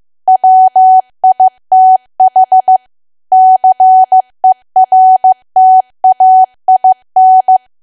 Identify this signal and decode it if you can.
Morse code, heard as a keyed tone, about 15 wpm, 750 Hz. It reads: WITH CERTAIN
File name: cwa / W1AW-Level2 / 15WPM